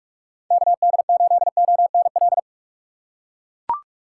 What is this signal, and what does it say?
Morse code, heard as a keyed tone, about 41 wpm, 700 Hz. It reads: KD8ONL
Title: USA Call Signs - 6 Characters - 2x Character Spacing - 45wpm